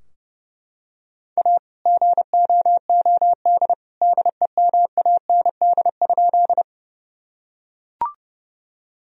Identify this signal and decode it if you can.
Morse code, heard as a keyed tone, about 32 wpm, 700 Hz. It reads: A GOOD DEMAND?
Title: Sentences from Top 1000 Words - Review Part 4 - 30wpm